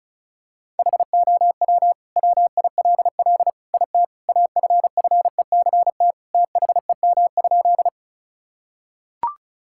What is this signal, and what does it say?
Morse code, heard as a keyed tone, about 36 wpm, 700 Hz. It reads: HOW WILL IT AFFECT THEM?
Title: Sentences from Top 1000 Words - Review Part 6 - 35wpm